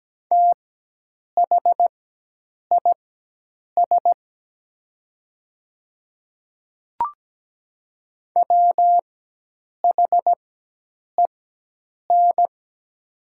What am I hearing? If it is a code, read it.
Morse code, heard as a keyed tone, about 17 wpm, spaced as for 8 wpm, 700 Hz. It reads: THIS WHEN
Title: 4x Character Spacing - Top 100 Words - 17wpm